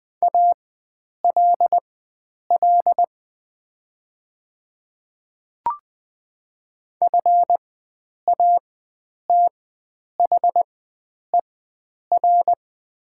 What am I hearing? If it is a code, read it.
Morse code, heard as a keyed tone, about 20 wpm, spaced as for 9 wpm, 700 Hz. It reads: ALL FATHER
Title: 4x Character Spacing - Top 300 Words - 20wpm